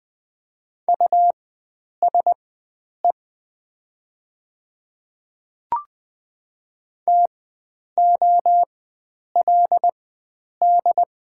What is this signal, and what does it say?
Morse code, heard as a keyed tone, about 20 wpm, spaced as for 9 wpm, 700 Hz. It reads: USE TOLD
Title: 4x Character Spacing - Top 500 Words - 20wpm